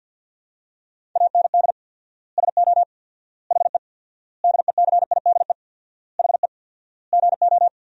Morse code, heard as a keyed tone, about 52 wpm, 700 Hz. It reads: AND SO HE DECIDE HE GO
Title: 4x Word-Spacing - Sentences from Top 500 Words - 50wpm